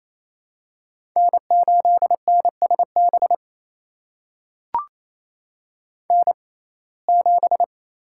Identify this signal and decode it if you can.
Morse code, heard as a keyed tone, about 28 wpm, spaced as for 20 wpm, 700 Hz. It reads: N8NSB N7
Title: USA Call Signs - 5 Characters - 6x Character Spacing - 28wpm